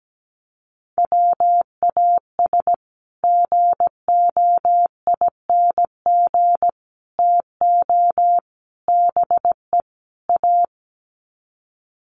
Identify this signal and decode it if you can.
Morse code, heard as a keyed tone, about 17 wpm, 700 Hz. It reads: WAS GOING TO BE A
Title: Sets of 5 Words 17wpm